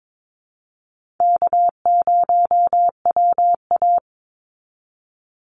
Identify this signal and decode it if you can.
Morse code, heard as a keyed tone, about 22 wpm, 700 Hz. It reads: K0WA
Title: Call Signs - 22wpm